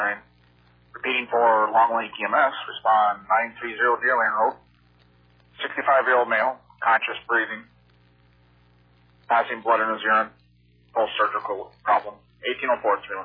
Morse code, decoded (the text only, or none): none